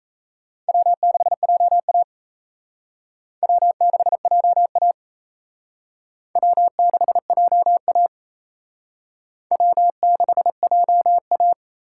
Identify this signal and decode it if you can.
Morse code, heard as a keyed tone, about 35 wpm, 700 Hz. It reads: W6JA W6JA W6JA W6JA
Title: Speed Racing - USA 1x2 Call Signs 28wpm